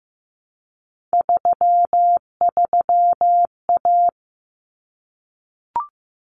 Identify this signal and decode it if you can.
Morse code, heard as a keyed tone, about 15 wpm, 700 Hz. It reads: 33A
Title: Common Cut Numbers - 3-Digit 15wpm